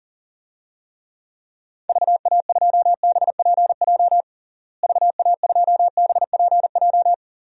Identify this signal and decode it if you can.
Morse code, heard as a keyed tone, about 40 wpm, 700 Hz. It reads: VA2BPJ VA2BPJ
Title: DXCC-100 Call Signs - 6 Characters - Repeated 3x - 40wpm